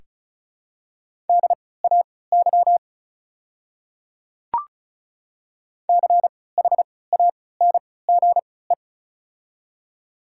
Morse code, heard as a keyed tone, about 36 wpm, spaced as for 20 wpm, 700 Hz. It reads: DAY CHANGE
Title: 3x Character Spacing - Top 200 Words - 35wpm